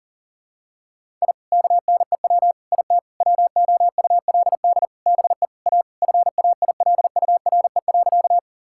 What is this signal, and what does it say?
Morse code, heard as a keyed tone, about 37 wpm, 700 Hz. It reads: I KNEW IT WOULD BE A FAILURE.
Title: Sentences from Top 1000 Words - Review Part 6 - 40wpm